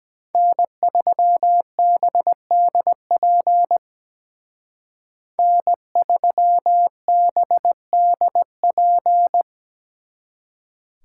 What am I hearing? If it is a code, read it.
Morse code, heard as a keyed tone, about 22 wpm, 700 Hz. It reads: N3BDP N3BDP